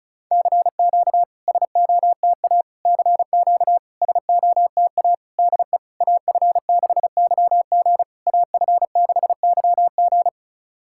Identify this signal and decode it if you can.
Morse code, heard as a keyed tone, about 36 wpm, 660 Hz. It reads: CQ SOTA CQ SOTA DE AF6YG AF6YG
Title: SOTA QSO Practice 35wpm